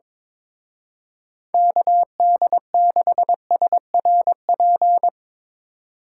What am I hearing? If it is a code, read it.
Morse code, heard as a keyed tone, about 22 wpm, 700 Hz. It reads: KD6SRP
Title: Speed Racing - USA 2x3 Call Signs 22wpm